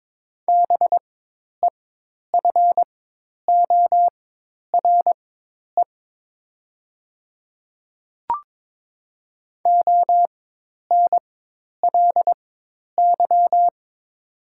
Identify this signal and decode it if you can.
Morse code, heard as a keyed tone, about 22 wpm, spaced as for 10 wpm, 700 Hz. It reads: BEFORE ONLY
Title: 4x Character Spacing - Top 300 Words - 22wpm